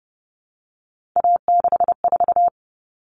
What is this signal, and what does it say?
Morse code, heard as a keyed tone, about 30 wpm, 700 Hz. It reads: A64
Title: Common Cut Numbers - 3-Digit 30wpm